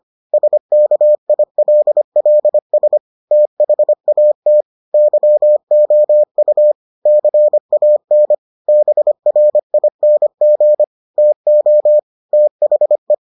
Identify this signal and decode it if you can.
Morse code, heard as a keyed tone, about 25 wpm, 600 Hz. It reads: SKILLS THAT YOU CAN BRING TO THE